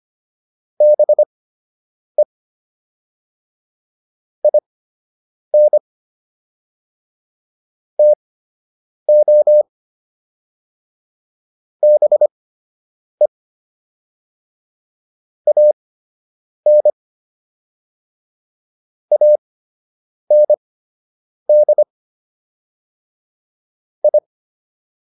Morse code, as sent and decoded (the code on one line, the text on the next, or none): -... / .. -. / - --- / -... / .- -. / .- -. -.. / ..
BE IN TO BE AN AND I